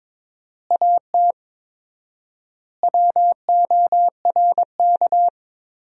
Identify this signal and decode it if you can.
Morse code, heard as a keyed tone, about 21 wpm, 700 Hz. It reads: AT WORK
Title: 4x Word-Spacing - Sentences from Top 200 Words - 22wpm